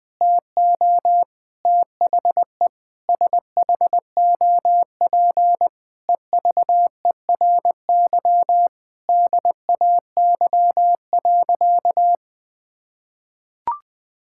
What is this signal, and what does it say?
Morse code, heard as a keyed tone, about 20 wpm, 700 Hz. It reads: TO THE SHOP EVERY DAY.
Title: Sentences from Top 1600 Words 20wpm